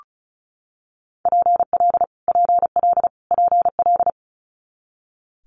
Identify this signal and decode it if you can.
Morse code, heard as a keyed tone, about 34 wpm, 700 Hz. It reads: PL PL PL
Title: Multi-letter Phonemes Repeated 3x 35wpm